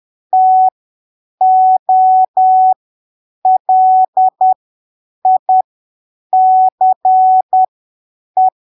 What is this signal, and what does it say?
Morse code, heard as a keyed tone, about 10 wpm, spaced as for 7 wpm, 750 Hz. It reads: TOLICE
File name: arquivos / texto / texto 15 10ppm